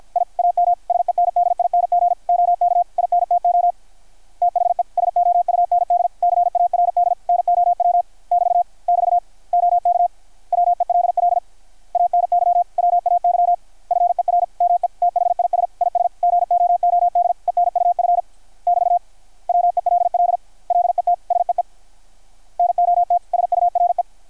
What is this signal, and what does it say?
Morse code, heard as a keyed tone, about 53 wpm, 720 Hz. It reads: I AM SENDING OK INTO THE SOUND CARD NOW <BT> <BT> OK WELL ANY WAY FER ME THIS IS GOOD ENUF <BT> WELL LET SEE NOT SURE
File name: archived / CRSnet